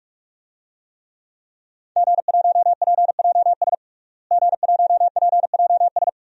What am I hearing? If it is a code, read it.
Morse code, heard as a keyed tone, about 46 wpm, 700 Hz. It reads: G1PJS G1PJS
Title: DXCC-100 Call Signs - 5 Characters - Repeated 3x - 45wpm